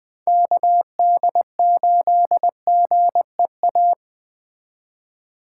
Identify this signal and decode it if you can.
Morse code, heard as a keyed tone, about 20 wpm, 700 Hz. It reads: KD8GEA
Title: Speed Racing - USA 2x3 Call Signs 17wpm